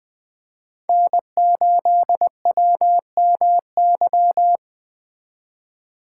Morse code, heard as a keyed tone, about 19 wpm, 700 Hz. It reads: N8WMY